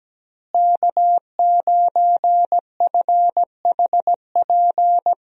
Morse code, heard as a keyed tone, about 17 wpm, 700 Hz. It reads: K9FHP
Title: USA Call Signs - 5 Characters - Repeated 3x - 17wpm